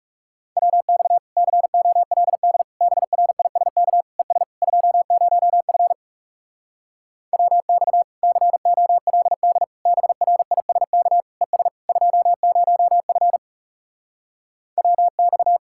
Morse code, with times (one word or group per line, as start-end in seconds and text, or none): 0.6-1.2 WX
1.4-2.6 COLD
2.8-4.0 BRISK
4.2-4.4 ES
4.6-5.9 20F
7.3-8.0 WX
8.2-9.6 COLD
9.9-11.2 BRISK
11.4-11.7 ES
11.9-13.4 20F
14.8-15.6 WX